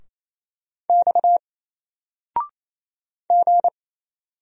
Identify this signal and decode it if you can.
Morse code, heard as a keyed tone, about 27 wpm, 700 Hz. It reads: X G
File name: Single-Letter-Number-28wpm